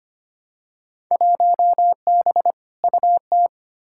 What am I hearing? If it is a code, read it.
Morse code, heard as a keyed tone, about 25 wpm, 700 Hz. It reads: 1B UT
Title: Field Day Exchanges - 25wpm